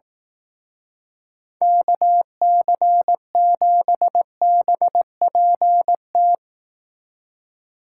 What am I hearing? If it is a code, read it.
Morse code, heard as a keyed tone, about 18 wpm, 700 Hz. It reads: KC7BPT
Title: Speed Racing - USA Call Signs 15wpm